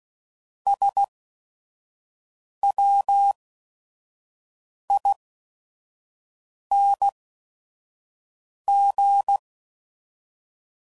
Morse code, heard as a keyed tone, about 16 wpm, spaced as for 5 wpm, 800 Hz. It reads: SWING
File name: Mixed / Misc / Words